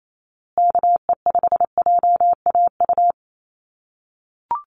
Morse code, heard as a keyed tone, about 28 wpm, 700 Hz. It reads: KE5JAU
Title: USA Call Signs Repeated 2x 28wpm